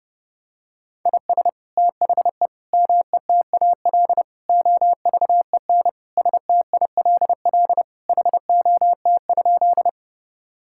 Morse code, heard as a keyed tone, about 30 wpm, 700 Hz. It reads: IS THE METAL OVEN STILL HOT?
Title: Sentences from Top 1000 Words - Review Part 6 - 30wpm